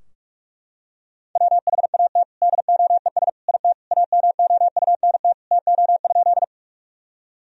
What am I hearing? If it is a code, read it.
Morse code, heard as a keyed tone, about 44 wpm, 700 Hz. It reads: WHAT DOES IT AMOUNT TO?
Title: Sentences from Top 1000 Words - Review Part 4 - 45wpm